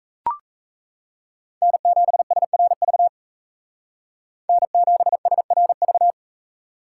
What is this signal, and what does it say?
Morse code, heard as a keyed tone, about 35 wpm, 700 Hz. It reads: N7SRV N7SRV